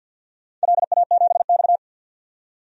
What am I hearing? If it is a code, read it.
Morse code, heard as a keyed tone, about 51 wpm, 700 Hz. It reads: LA7X
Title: DXCC-100 Call Signs - 4 Characters - 50wpm